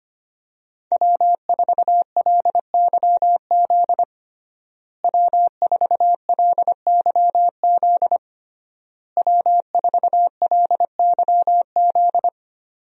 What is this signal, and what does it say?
Morse code, heard as a keyed tone, about 25 wpm, 700 Hz. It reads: W4LYZ W4LYZ W4LYZ
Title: USA Call Signs - 5 Characters - Repeated 3x - 25wpm